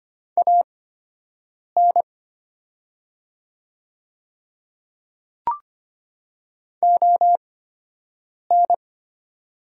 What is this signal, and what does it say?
Morse code, heard as a keyed tone, about 25 wpm, spaced as for 7 wpm, 700 Hz. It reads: AN ON
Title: 8x Character Spacing - Top 500 Words - 25wpm